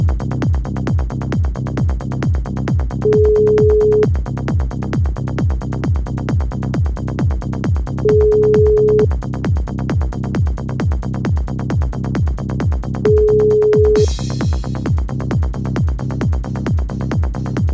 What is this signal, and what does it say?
Ringback tone (Europe, 425 Hz), with noise.